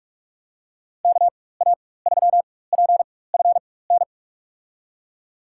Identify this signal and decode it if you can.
Morse code, heard as a keyed tone, about 44 wpm, spaced as for 21 wpm, 700 Hz. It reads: KA3PFN
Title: USA Call Signs - 6 Characters - 4x Character Spacing - 45wpm